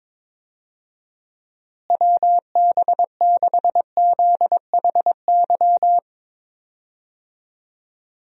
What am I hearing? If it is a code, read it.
Morse code, heard as a keyed tone, about 22 wpm, 700 Hz. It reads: WB6ZHY
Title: USA Call Signs - 6 Characters - Repeated 3x - 22wpm